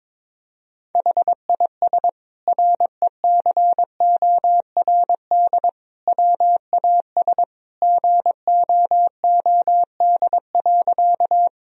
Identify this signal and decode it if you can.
Morse code, heard as a keyed tone, about 22 wpm, 700 Hz. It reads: HIS RECORD WAS GOOD.
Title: Sentences from Top 1000 Words - Review Part 3 - 22wpm